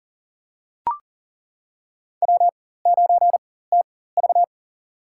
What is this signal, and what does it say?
Morse code, heard as a keyed tone, about 40 wpm, spaced as for 19 wpm, 700 Hz. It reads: W9TV
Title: USA Call Signs - 4 Characters - 4x Character Spacing - 40wpm